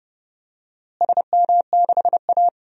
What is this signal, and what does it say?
Morse code, heard as a keyed tone, about 30 wpm, 700 Hz. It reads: SM6A
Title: DXCC-100 Call Signs - 4 Characters - Repeated 2x - 30wpm